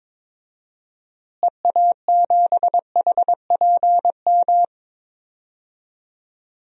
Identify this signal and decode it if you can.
Morse code, heard as a keyed tone, about 23 wpm, 700 Hz. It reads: EA7HPM
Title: DXCC-100 Call Signs - 6 Characters - Repeated 3x - 22wpm